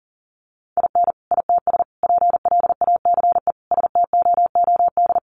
Morse code, heard as a keyed tone, about 37 wpm, 700 Hz. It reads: IN ITS PLACE STOOD
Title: Sentences from Top 1400 Words 40wpm